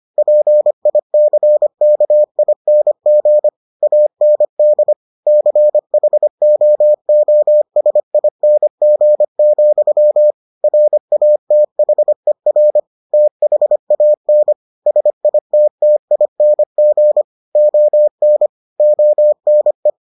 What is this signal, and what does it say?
Morse code, heard as a keyed tone, about 25 wpm, 600 Hz. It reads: PICKING AND CHOOSING, RATHER THAN SITTING ON ONE